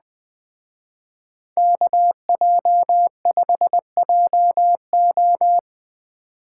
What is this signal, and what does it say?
Morse code, heard as a keyed tone, about 20 wpm, 700 Hz. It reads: KJ5JO
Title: Speed Racing - USA 2x2 Call Signs 17wpm